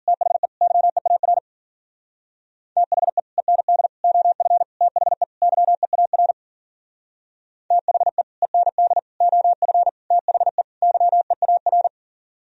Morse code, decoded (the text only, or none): THE YEAR THE END OF THE YEAR THE END OF THE YEAR